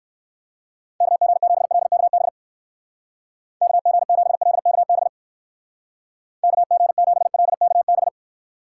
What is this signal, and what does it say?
Morse code, heard as a keyed tone, about 59 wpm, 700 Hz. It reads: KG7LKB KG7LKB KG7LKB